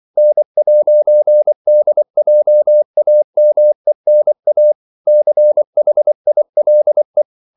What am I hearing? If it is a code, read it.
Morse code, heard as a keyed tone, about 24 wpm, 600 Hz. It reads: N'DJAMENA CHILE
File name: CW / Practice / NationalCapitals24